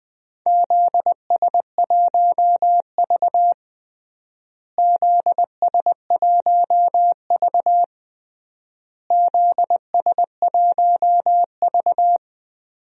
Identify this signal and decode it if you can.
Morse code, heard as a keyed tone, about 20 wpm, 700 Hz. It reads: ZS1V ZS1V ZS1V
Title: DXCC-100 Call Signs - 4 Characters - Repeated 3x - 20wpm